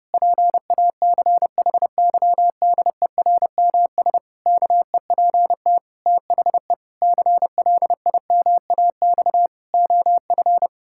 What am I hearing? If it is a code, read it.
Morse code, heard as a keyed tone, about 30 wpm, 700 Hz. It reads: PACHYDERMS KEPT THE CLIMAX OF
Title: Pangram Challenge v2 - 30wpm